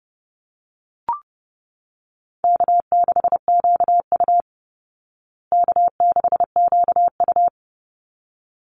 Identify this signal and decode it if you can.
Morse code, heard as a keyed tone, about 30 wpm, 700 Hz. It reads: K6QU K6QU
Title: Easy Call Signs Repeated 2x - 30wpm